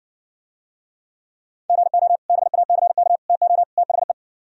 Morse code, heard as a keyed tone, about 60 wpm, 700 Hz. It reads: GO BACK TO THE